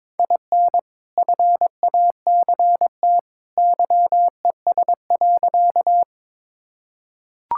Morse code, heard as a keyed tone, about 22 wpm, 700 Hz. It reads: IN FACT YES.
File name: practice / 22wpm / Sentences with 2-Word Phrases - 22wpm